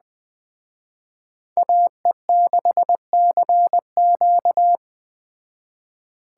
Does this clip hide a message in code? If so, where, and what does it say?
Yes: Morse code from 1.6 to 4.8 s, heard as a keyed tone, about 20 wpm, 700 Hz. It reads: AE6CQ